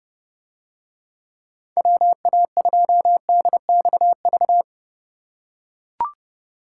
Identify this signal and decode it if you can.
Morse code, heard as a keyed tone, about 29 wpm, 700 Hz. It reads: WA2DXV